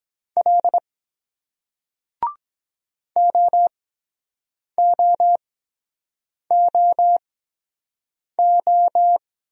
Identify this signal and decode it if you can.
Morse code, heard as a keyed tone, about 23 wpm, spaced as for 7 wpm, 700 Hz. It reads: L OOOO